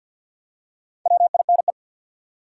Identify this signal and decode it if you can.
Morse code, heard as a keyed tone, about 49 wpm, 700 Hz. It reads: WINE
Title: Four Letter Words - 50wpm